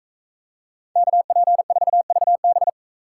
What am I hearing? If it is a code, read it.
Morse code, heard as a keyed tone, about 44 wpm, 700 Hz. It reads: KP4VB